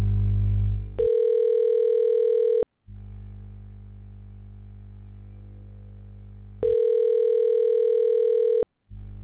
Ringback tone (North America), 440 + 480 Hz.